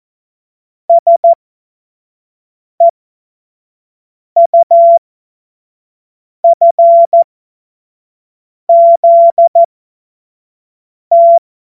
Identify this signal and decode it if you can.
Morse code, heard as a keyed tone, about 14 wpm, spaced as for 5 wpm, 680 Hz. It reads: SEUFZT
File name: Morsekurs / Witze / Witz 16 Januar 2024 25 F70